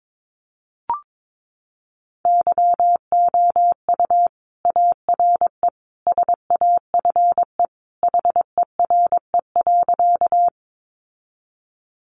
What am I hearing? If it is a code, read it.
Morse code, heard as a keyed tone, about 22 wpm, 700 Hz. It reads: YOU ARE SAFE HERE.
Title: Sentences from Top 900 Words 22wpm